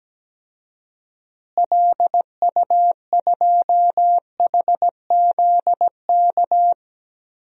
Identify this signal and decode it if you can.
Morse code, heard as a keyed tone, about 17 wpm, 700 Hz. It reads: LU2HZK
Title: DXCC-100 Call Signs - 6 Characters - Repeated 3x - 17wpm